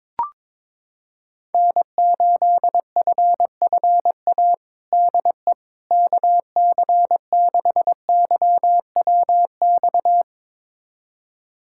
Morse code, heard as a keyed tone, about 22 wpm, 700 Hz. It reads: N8FFA DE KC6YWX
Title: USA Callsign DE Callsign 22wpm